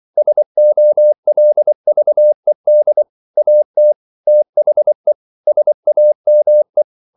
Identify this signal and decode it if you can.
Morse code, heard as a keyed tone, about 24 wpm, 600 Hz. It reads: SOLVED AT THE SAME
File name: CW / Practice / EnglishQuotations0224